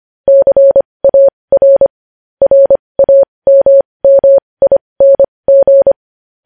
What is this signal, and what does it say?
Morse code, heard as a keyed tone, about 25 wpm, spaced as for 20 wpm, 550 Hz. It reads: CAR RAMMING
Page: news headlines in morse code at 20 wpm